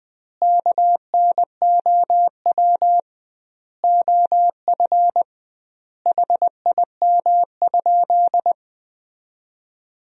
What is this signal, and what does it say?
Morse code, heard as a keyed tone, about 20 wpm, 700 Hz. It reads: KNOW OF HIM?
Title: 2x Word-Spacing - Sentences from Top 100 Words - 20wpm